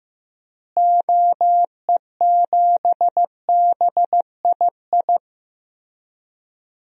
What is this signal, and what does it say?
Morse code, heard as a keyed tone, about 16 wpm, 700 Hz. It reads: OE7BII
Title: DXCC-100 Call Signs - 6 Characters - Repeated 3x - 15wpm